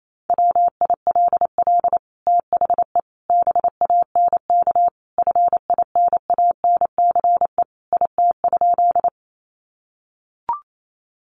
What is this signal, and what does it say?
Morse code, heard as a keyed tone, about 28 wpm, 700 Hz. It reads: WILL THE BANK FINANCE IT?
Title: Sentences from Top 1500 Words 28wpm